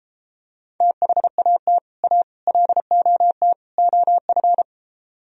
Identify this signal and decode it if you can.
Morse code, heard as a keyed tone, about 33 wpm, 700 Hz. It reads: THAT A LOT OF